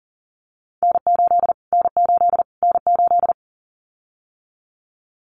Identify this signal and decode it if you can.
Morse code, heard as a keyed tone, about 40 wpm, 700 Hz. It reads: N8 N8 N8
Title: Callsign Prefixes for Top 25 DXCC - Repeated 3x - 40wpm